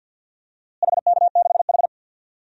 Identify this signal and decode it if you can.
Morse code, heard as a keyed tone, about 52 wpm, 700 Hz. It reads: HK6H
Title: DXCC-100 Call Signs - 4 Characters - Repeated 2x - 50wpm